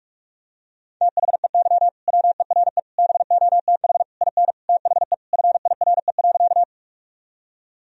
Morse code, heard as a keyed tone, about 41 wpm, 700 Hz. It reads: THEY WERE BOTH IN THE FIRE.
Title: Sentences from Top 1000 Words - Review Part 1 - 45wpm